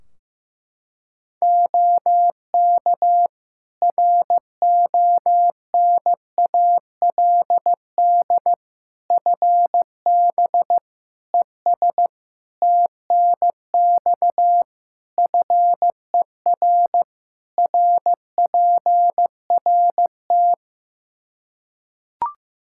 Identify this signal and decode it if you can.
Morse code, heard as a keyed tone, about 15 wpm, 700 Hz. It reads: OK RONALD FB ES TNX FER RPRT
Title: Practice QSO - Exchange 2 - Info - 15wpm